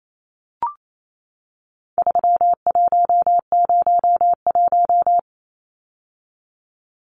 Morse code, heard as a keyed tone, about 28 wpm, 700 Hz. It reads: 3101
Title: Numbers - 4-Digit 28wpm